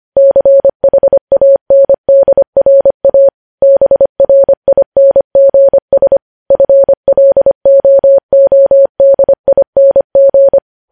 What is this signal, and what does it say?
Morse code, heard as a keyed tone, about 25 wpm, 550 Hz. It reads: CHANDRA BRINGS FLOODING
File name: podcast / News headlines-1x-25x25